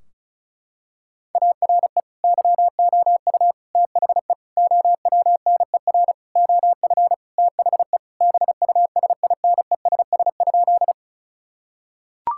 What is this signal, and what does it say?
Morse code, heard as a keyed tone, about 36 wpm, 700 Hz. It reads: ARE YOU THE OWNER OF THE BUSINESS?
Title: Sentences from Top 1600 Words 35wpm